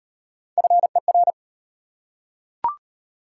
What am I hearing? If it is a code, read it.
Morse code, heard as a keyed tone, about 37 wpm, 700 Hz. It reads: FER